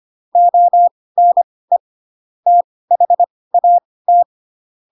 Morse code, heard as a keyed tone, about 25 wpm, spaced as for 18 wpm, 700 Hz. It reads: ONE THAT
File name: Pinocchio / Pinocchio ch2 12 25 ik1hge